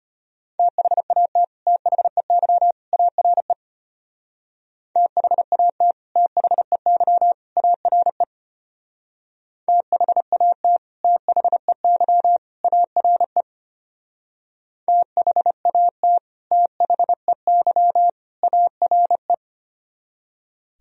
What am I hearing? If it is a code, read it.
Morse code, heard as a keyed tone, about 31 wpm, 700 Hz. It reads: THAT THEY ARE THAT THEY ARE THAT THEY ARE THAT THEY ARE